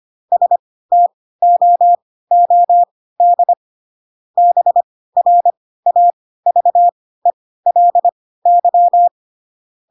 Morse code, heard as a keyed tone, about 25 wpm, spaced as for 16 wpm, 700 Hz. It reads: STOOD BRAVELY
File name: Pinocchio / Pinocchio ch3 10 25 ik1hge